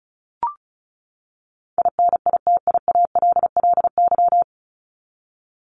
Morse code, heard as a keyed tone, about 34 wpm, 700 Hz. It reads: INITIALLY
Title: Top 1251-1500 Words 35wpm